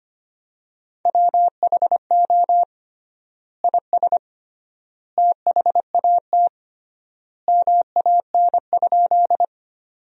Morse code, heard as a keyed tone, about 25 wpm, 700 Hz. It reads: WHO IS THAT MAN?
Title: 3x Word-Spacing - Sentences from Top 200 Words - 25wpm